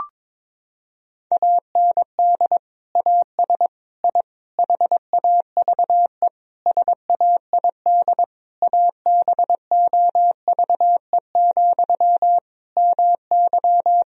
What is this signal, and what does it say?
Morse code, heard as a keyed tone, about 22 wpm, 700 Hz. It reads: AND AS I HAVE SAID ABOVE, MY
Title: Sentences from Top 1000 Words - Review Part 2 - 22wpm